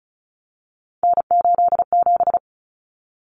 Morse code, heard as a keyed tone, about 34 wpm, 700 Hz. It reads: N87
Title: Common Cut Numbers - 3-Digit 35wpm